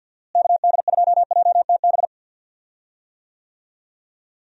Morse code, heard as a keyed tone, about 49 wpm, 700 Hz. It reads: KD2JTB
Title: USA Call Signs - 6 Characters - Repeated 3x - 50wpm